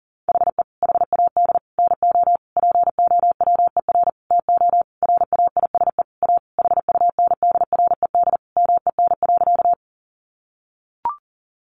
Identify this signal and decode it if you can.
Morse code, heard as a keyed tone, about 40 wpm, 700 Hz. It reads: HE HAD NO POWER TO RAISE A HUNDRED MEN.
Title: Sentences from Top 1400 Words 40wpm